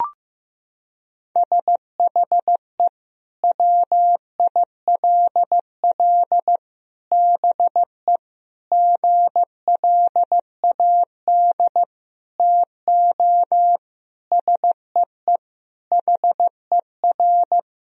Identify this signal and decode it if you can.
Morse code, heard as a keyed tone, about 15 wpm, 700 Hz. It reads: SHE WILL BE GLAD TO SEE HER
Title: Sentences from Top 1000 Words - Review Part 2 - 15wpm